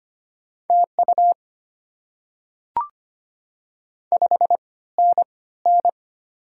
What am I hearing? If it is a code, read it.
Morse code, heard as a keyed tone, about 25 wpm, spaced as for 16 wpm, 700 Hz. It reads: TU 5NN
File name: Top 20 QSO Elements - 3x Character Spacing - 25wpm